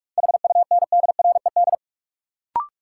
Morse code, heard as a keyed tone, about 41 wpm, 700 Hz. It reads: HUNDRED
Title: Sentences from Top 400 Words - New Word Drill - 45wpm